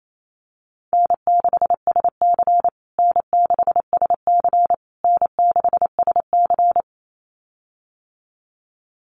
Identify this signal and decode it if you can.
Morse code, heard as a keyed tone, about 28 wpm, 700 Hz. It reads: N6SC N6SC N6SC
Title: USA Call Signs Repeated 3x 28wpm